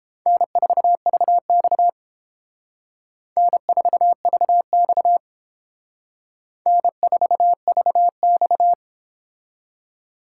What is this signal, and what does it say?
Morse code, heard as a keyed tone, about 28 wpm, 700 Hz. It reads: N4VX N4VX N4VX